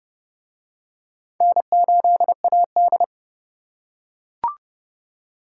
Morse code, heard as a keyed tone, about 26 wpm, 700 Hz. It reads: N8AD